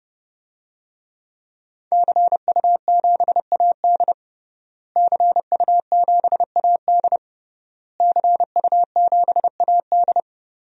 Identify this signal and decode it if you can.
Morse code, heard as a keyed tone, about 30 wpm, 700 Hz. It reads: CU7AD CU7AD CU7AD
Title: DXCC-100 Call Signs - 5 Characters - Repeated 3x - 30wpm